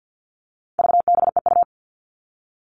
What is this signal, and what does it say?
Morse code, heard as a keyed tone, about 50 wpm, 700 Hz. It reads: VBEU
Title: All Cut Numbers - 4-Digit 50wpm